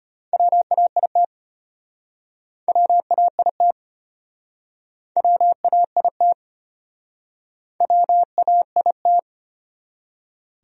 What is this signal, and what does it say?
Morse code, heard as a keyed tone, about 36 wpm, 700 Hz. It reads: WAIT WAIT WAIT WAIT